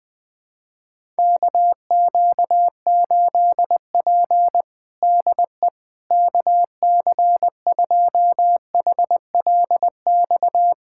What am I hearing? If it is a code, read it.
Morse code, heard as a keyed tone, about 20 wpm, 700 Hz. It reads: KQ8P DE KC2HLX
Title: USA Callsign DE Callsign 20wpm